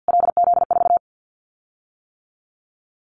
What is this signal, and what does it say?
Morse code, heard as a keyed tone, about 50 wpm, 700 Hz. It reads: LZ4
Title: Callsign Prefixes for Top 25 DXCC - Repeated 3x - 50wpm